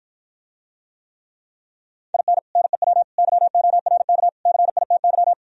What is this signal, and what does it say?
Morse code, heard as a keyed tone, about 52 wpm, 700 Hz. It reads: IN NEW YORK CITY